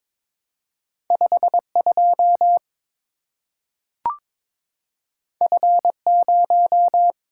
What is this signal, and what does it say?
Morse code, heard as a keyed tone, about 22 wpm, 700 Hz. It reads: 52 F0
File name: practice / 22wpm / Two Letter-Number Combinations - 22wpm